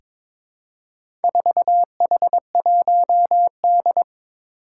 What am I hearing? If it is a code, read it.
Morse code, heard as a keyed tone, about 22 wpm, 700 Hz. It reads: 4H1D
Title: Four Challenging Letter-Number Combinations 22wpm